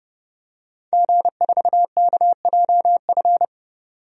Morse code, heard as a keyed tone, about 30 wpm, 700 Hz. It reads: G4KJF
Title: DXCC-100 Call Signs - 5 Characters - 30wpm